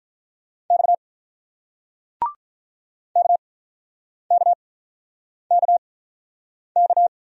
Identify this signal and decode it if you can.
Morse code, heard as a keyed tone, about 48 wpm, spaced as for 9 wpm, 700 Hz. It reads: X KKKK